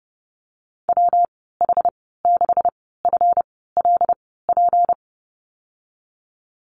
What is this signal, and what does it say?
Morse code, heard as a keyed tone, about 32 wpm, spaced as for 17 wpm, 700 Hz. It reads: WH6FLP
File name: Lesson 37g - Letter x - Call Signs 30wpm